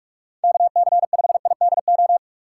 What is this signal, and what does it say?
Morse code, heard as a keyed tone, about 46 wpm, 700 Hz. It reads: KC5IDO
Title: USA Call Signs - 6 Characters - 6x Character Spacing - 45wpm